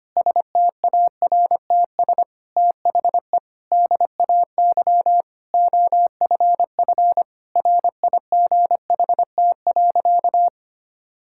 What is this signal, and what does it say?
Morse code, heard as a keyed tone, about 25 wpm, 700 Hz. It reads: STARTS THE DAY OFF RIGHT.
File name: practice / 25wpm / Difficult Sentences with Hints - 25wpm